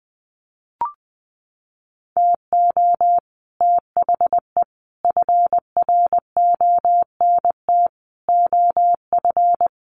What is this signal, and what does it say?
Morse code, heard as a keyed tone, about 20 wpm, 700 Hz. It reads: TO THE FRONT OF
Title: Sets of 5 Words 20wpm